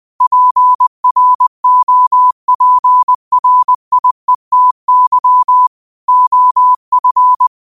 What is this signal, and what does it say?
Morse code, heard as a keyed tone, about 20 wpm, 1000 Hz. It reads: PROPRIETY OF
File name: qotdfiles / 20wpm